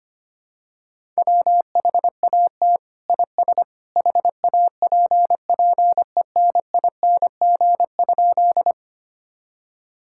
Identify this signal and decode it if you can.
Morse code, heard as a keyed tone, about 25 wpm, 700 Hz. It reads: WHAT IS HAPPENING?
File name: practice / 25wpm / Sentences with 2-Word Phrases - 25wpm